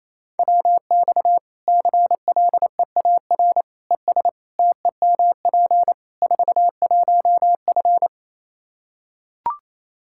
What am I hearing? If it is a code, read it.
Morse code, heard as a keyed tone, about 28 wpm, 700 Hz. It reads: WX CLEAR ES TEMP 41F
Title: K1USN SST QSO Practice 28wpm